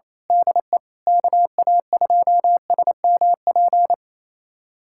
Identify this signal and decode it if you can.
Morse code, heard as a keyed tone, about 28 wpm, 700 Hz. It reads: DE KA2SMP
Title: USA Callsign DE Callsign - 6 Characters - 28wpm